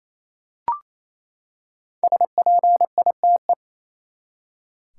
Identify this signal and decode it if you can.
Morse code, heard as a keyed tone, about 28 wpm, 700 Hz. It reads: SPITE
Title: Five Letter Words - 28wpm